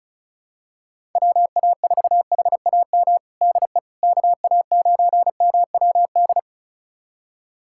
Morse code, heard as a keyed tone, about 34 wpm, 700 Hz. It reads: WA4HAM DE KA9MWD
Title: USA Callsign DE Callsign - 6 Characters - 35wpm